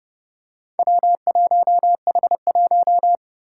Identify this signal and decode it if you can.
Morse code, heard as a keyed tone, about 32 wpm, 700 Hz. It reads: W1H1
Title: Four Challenging Letter-Number Combinations 30wpm